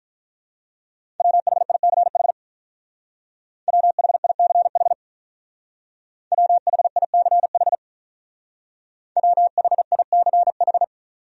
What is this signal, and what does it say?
Morse code, heard as a keyed tone, about 43 wpm, 700 Hz. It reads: WHICH WHICH WHICH WHICH
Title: Speed-Racing Top 200 Words 35wpm